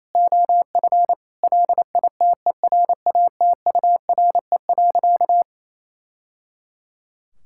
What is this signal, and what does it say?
Morse code, heard as a keyed tone, about 28 wpm, 700 Hz. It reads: OF LITERATURE.
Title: Sentences from Top 1000 Words - Review Part 6 - 28wpm